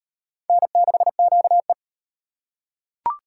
Morse code, heard as a keyed tone, about 32 wpm, 700 Hz. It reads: N6QE